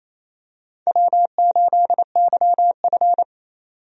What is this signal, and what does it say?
Morse code, heard as a keyed tone, about 28 wpm, 700 Hz. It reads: W8YF
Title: USA Call Signs - 4 Characters - 28wpm